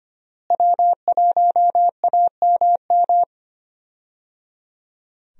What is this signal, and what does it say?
Morse code, heard as a keyed tone, about 25 wpm, 700 Hz. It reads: W1AMM
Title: USA Call Signs - 5 Characters - Repeated 3x - 25wpm